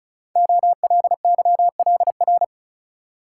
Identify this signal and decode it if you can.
Morse code, heard as a keyed tone, about 36 wpm, 700 Hz. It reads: OLYLR
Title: Code Groups - 5 Letter:Numbers - 35wpm